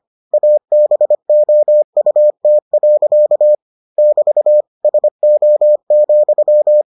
Morse code, heard as a keyed tone, about 25 wpm, 600 Hz. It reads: ABOUT. <BT> SO,